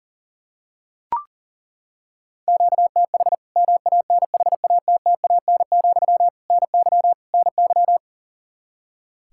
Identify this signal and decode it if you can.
Morse code, heard as a keyed tone, about 40 wpm, 700 Hz. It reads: QTH MANHATTAN, NY NY
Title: Practice QSO - Exchange 1 - Beginning - 40wpm